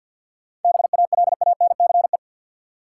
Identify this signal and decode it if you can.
Morse code, heard as a keyed tone, about 48 wpm, 700 Hz. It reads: BALANCE
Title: Sentences from Top 900 Words - New Word Drill - 50wpm